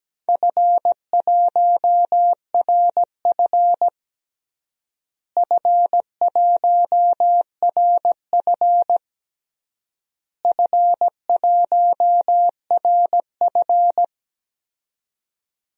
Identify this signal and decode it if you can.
Morse code, heard as a keyed tone, about 17 wpm, 700 Hz. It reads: F1RF F1RF F1RF
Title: DXCC-100 Call Signs - 4 Characters - Repeated 3x - 17wpm